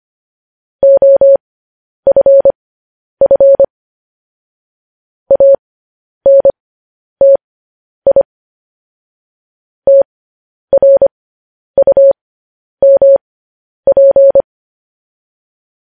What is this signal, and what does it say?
Morse code, heard as a keyed tone, about 25 wpm, spaced as for 10 wpm, 550 Hz. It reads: OFF ANTI TRUMP
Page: news headlines in morse code at 10 wpm